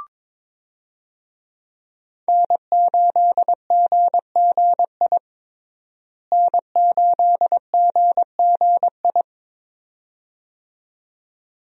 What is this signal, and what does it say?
Morse code, heard as a keyed tone, about 22 wpm, 700 Hz. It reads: N8GGI N8GGI
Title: USA Call Signs - 5 Characters - Repeated 2x - 22wpm